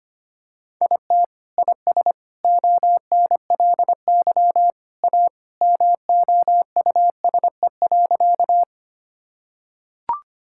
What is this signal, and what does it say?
Morse code, heard as a keyed tone, about 25 wpm, 700 Hz. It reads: IT IS ONLY A MOUSE.
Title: Sentences from Top 1000 Words - Review Part 5 - 25wpm